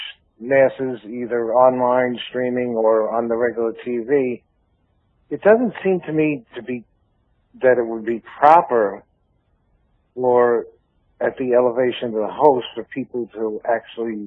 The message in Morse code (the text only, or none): none